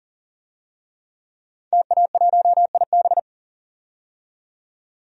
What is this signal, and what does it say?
Morse code, heard as a keyed tone, about 40 wpm, 700 Hz. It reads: TA1IB
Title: DXCC-100 Call Signs - 5 Characters - 40wpm